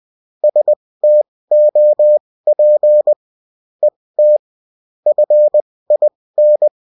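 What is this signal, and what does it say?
Morse code, heard as a keyed tone, about 20 wpm, spaced as for 16 wpm, 600 Hz. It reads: STOP ET FIN
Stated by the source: Morse code bulletin of a French amateur CW union, sent as text and in CW.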